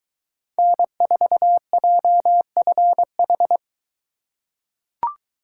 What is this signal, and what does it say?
Morse code, heard as a keyed tone, about 21 wpm, 700 Hz. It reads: N4JFH